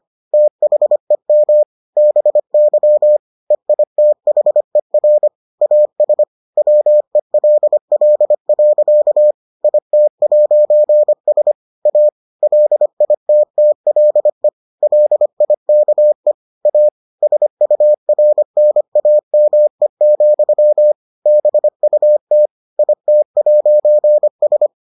Morse code, heard as a keyed tone, about 25 wpm, 600 Hz. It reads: THEM BY EITHER AS WELL. IT'S A LITTLE LIKE A SURNAME, BUT IT'S